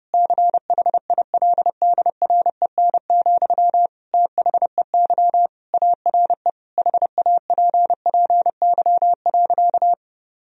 Morse code, heard as a keyed tone, about 30 wpm, 700 Hz. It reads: CHILDREN, THEY ARE HAPPY.
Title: Sentences from Top 1000 Words - Review Part 3 - 30wpm